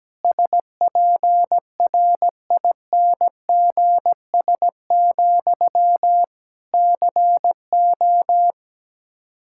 Morse code, heard as a keyed tone, about 17 wpm, 700 Hz. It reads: SPRINGS, CO
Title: Practice QSO - Exchange 1 - Beginning - 17wpm